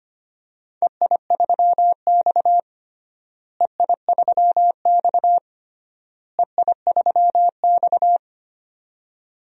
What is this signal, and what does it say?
Morse code, heard as a keyed tone, about 25 wpm, 700 Hz. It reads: EI3X EI3X EI3X
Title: DXCC-100 Call Signs - 4 Characters - Repeated 3x - 25wpm